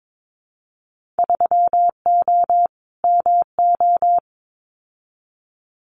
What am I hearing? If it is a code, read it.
Morse code, heard as a keyed tone, about 22 wpm, 700 Hz. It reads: 3O MO
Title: Winter Field Day Exchanges Encore 22wpm